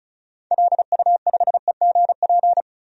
Morse code, heard as a keyed tone, about 36 wpm, 700 Hz. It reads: LU5EGP
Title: DXCC-100 Call Signs - 6 Characters - 35wpm